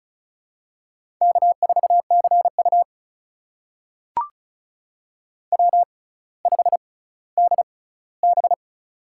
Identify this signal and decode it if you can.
Morse code, heard as a keyed tone, about 36 wpm, spaced as for 18 wpm, 700 Hz. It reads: K4CU W5DB
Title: USA Call Signs - 6x Character Spacing - 35wpm